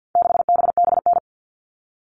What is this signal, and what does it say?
Morse code, heard as a keyed tone, about 50 wpm, 700 Hz. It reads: 6BBN
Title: All Cut Numbers - 4-Digit 50wpm